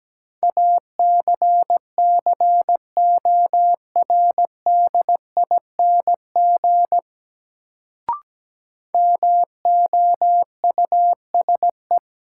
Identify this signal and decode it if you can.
Morse code, heard as a keyed tone, about 17 wpm, 700 Hz. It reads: ACCORDING MOUSE
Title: Sentences from Top 700 Words - New Word Drill - 17wpm